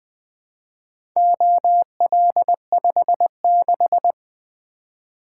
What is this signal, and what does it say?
Morse code, heard as a keyed tone, about 20 wpm, 700 Hz. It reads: OL56
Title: Four Letter-Number Combinations - 20wpm